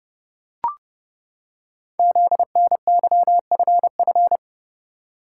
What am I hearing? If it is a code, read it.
Morse code, heard as a keyed tone, about 30 wpm, 700 Hz. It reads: ZNYFF
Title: Code Groups - 5 Letters - 30wpm